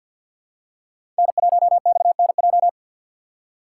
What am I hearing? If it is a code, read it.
Morse code, heard as a keyed tone, about 46 wpm, 700 Hz. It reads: N1XNJ
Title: USA Call Signs - 5 Characters - 2x Character Spacing - 50wpm